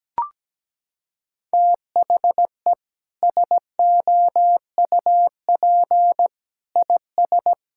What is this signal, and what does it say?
Morse code, heard as a keyed tone, about 17 wpm, 700 Hz. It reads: THE SOUP IS
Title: Sentences from Top 1000 Words - Review Part 6 - 17wpm